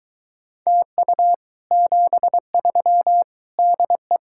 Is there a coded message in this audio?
Yes: Morse code from 0.7 to 4.2 s, heard as a keyed tone, about 23 wpm, 700 Hz. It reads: TU 73 DE